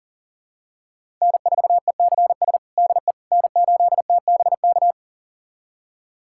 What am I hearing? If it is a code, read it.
Morse code, heard as a keyed tone, about 40 wpm, 700 Hz. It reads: N4ECS DE N8TBK
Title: USA Callsign DE Callsign - 5 Characters - 40wpm